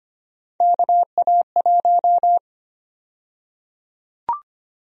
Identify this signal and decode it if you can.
Morse code, heard as a keyed tone, about 25 wpm, 700 Hz. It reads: KA1
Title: USA Call Sign Prefixes - 25wpm